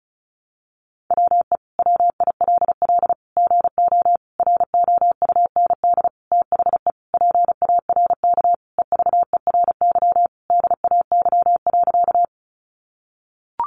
Morse code, heard as a keyed tone, about 34 wpm, 700 Hz. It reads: WE WILL GO ROUND THE PARK EVERY DAY.
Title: Sentences from Top 1300 Words 35wpm